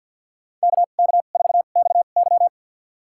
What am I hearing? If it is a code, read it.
Morse code, heard as a keyed tone, about 53 wpm, spaced as for 37 wpm, 700 Hz. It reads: KK4XY
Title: USA Call Signs - 5 Characters - 2x Character Spacing - 50wpm